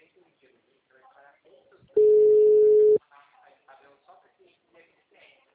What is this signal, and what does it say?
Ringback tone (Europe, 425 Hz), with speech.
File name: out- -2002- - - .13713